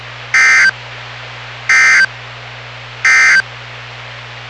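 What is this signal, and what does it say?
Emergency Alert System (SAME) tones, heard as harsh buzzing bursts. It is heard from 0.2 to 3.5 s.
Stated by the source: NOAA Weather Radio station WWG57 on 162.450 MHz, local weather broadcast.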